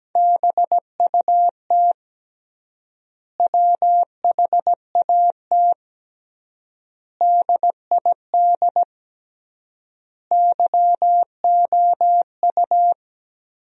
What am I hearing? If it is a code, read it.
Morse code, heard as a keyed tone, about 17 wpm, 700 Hz. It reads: BUT WHAT DID YOU
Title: 3x Word-Spacing - Sentences from Top 200 Words - 17wpm